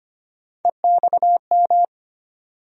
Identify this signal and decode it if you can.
Morse code, heard as a keyed tone, about 25 wpm, 700 Hz. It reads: EXM
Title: USA Call Sign Postfixes - 3 Letters 25wpm